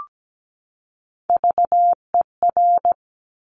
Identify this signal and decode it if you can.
Morse code, heard as a keyed tone, about 17 wpm, 700 Hz. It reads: VER
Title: Top Three Letter Combinations in Words 17wpm